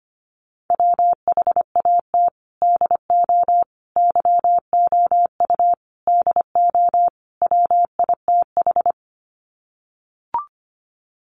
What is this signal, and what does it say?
Morse code, heard as a keyed tone, about 25 wpm, 700 Hz. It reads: WHAT DO YOU DO WITH
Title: Sets of 5 Words Encore2 25wpm